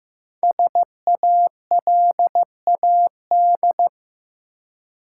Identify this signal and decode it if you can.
Morse code, heard as a keyed tone, about 15 wpm, 700 Hz. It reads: SALAD
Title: Five Letter Words - 15wpm